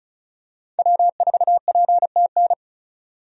Morse code, heard as a keyed tone, about 36 wpm, 700 Hz. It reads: W4PTN
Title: USA Call Signs - 5 Characters - 35wpm